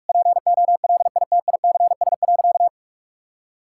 Morse code, heard as a keyed tone, about 42 wpm, 700 Hz. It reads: POLITICS.
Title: Sentences from Top 1000 Words - Review Part 6 - 45wpm